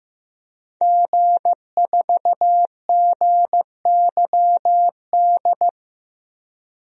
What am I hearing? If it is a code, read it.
Morse code, heard as a keyed tone, about 16 wpm, 700 Hz. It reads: G4GYD
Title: DXCC-100 Call Signs - 5 Characters - Repeated 2x - 15wpm